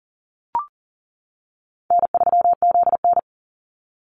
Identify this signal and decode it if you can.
Morse code, heard as a keyed tone, about 40 wpm, 700 Hz. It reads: N3ZN
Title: Easy Call Signs - 40wpm